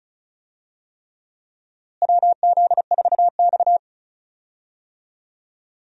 Morse code, heard as a keyed tone, about 36 wpm, 700 Hz. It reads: WZ4X
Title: USA Call Signs - 4 Characters - Repeated 3x - 35wpm